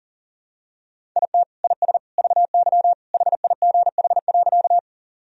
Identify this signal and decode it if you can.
Morse code, heard as a keyed tone, about 40 wpm, 700 Hz. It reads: IT IS VY HIGH.
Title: Sentences from Top 200 Words and Abbreviations 40wpm